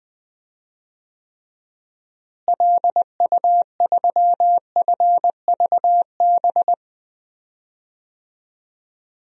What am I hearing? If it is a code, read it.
Morse code, heard as a keyed tone, about 20 wpm, 700 Hz. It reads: LU3FVB
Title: DXCC-100 Call Signs - 6 Characters - Repeated 3x - 20wpm